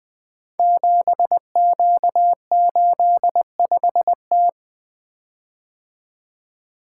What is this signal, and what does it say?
Morse code, heard as a keyed tone, about 20 wpm, 700 Hz. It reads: 7Q85T
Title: Code Groups - 5 Letter:Numbers - 20wpm